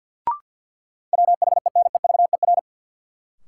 Morse code, heard as a keyed tone, about 53 wpm, 700 Hz. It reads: WHENEVER
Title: Sentences from Top 1200 Words - New Word Drill Alt - 50wpm